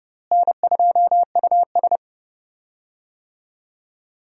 Morse code, heard as a keyed tone, about 30 wpm, 700 Hz. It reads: N2US
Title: USA Call Signs - 4 Characters - Repeated 3x - 30wpm